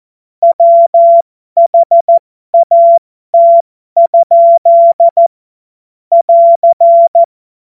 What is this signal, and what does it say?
Morse code, heard as a keyed tone, about 14 wpm, spaced as for 12 wpm, 680 Hz. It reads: WHAT? <AR>
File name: Morsekurs / Witze